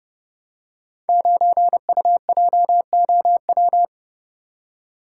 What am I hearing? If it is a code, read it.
Morse code, heard as a keyed tone, about 30 wpm, 700 Hz. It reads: 9UJOW
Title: Code Groups - 5 Letter:Numbers - 30wpm